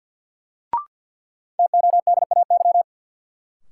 Morse code, heard as a keyed tone, about 53 wpm, 700 Hz. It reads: TODAY
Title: Sets of 5 Words Encore - New Word Drill - 50wpm